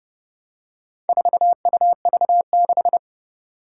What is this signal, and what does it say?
Morse code, heard as a keyed tone, about 30 wpm, 690 Hz. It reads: 4UV6
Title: Four Challenging Letter-Number Combinations 30wpm